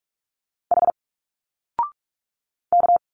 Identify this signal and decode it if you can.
Morse code, heard as a keyed tone, about 45 wpm, 700 Hz. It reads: H K
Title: Single Letters 45wpm